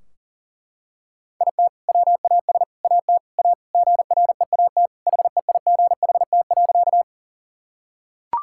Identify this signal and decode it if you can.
Morse code, heard as a keyed tone, about 40 wpm, 700 Hz. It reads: IT WAS AT A GREAT HEIGHT.
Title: Sentences from Top 1600 Words 40wpm